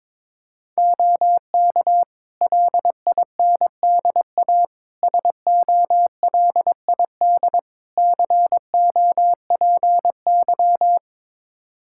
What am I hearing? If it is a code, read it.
Morse code, heard as a keyed tone, about 22 wpm, 700 Hz. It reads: OK LINDA SOLID COPY
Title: Practice QSO - Exchange 3 - More Info - 22wpm